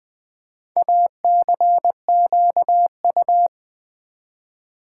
Morse code, heard as a keyed tone, about 20 wpm, 700 Hz. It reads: ACQU
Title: Build A Word - 4 Letters At A Time - 11-Letter Words - 20wpm